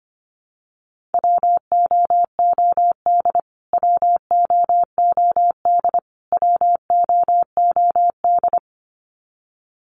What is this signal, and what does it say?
Morse code, heard as a keyed tone, about 25 wpm, 700 Hz. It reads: WOOD WOOD WOOD
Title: Top 300 Words Repeated 25wpm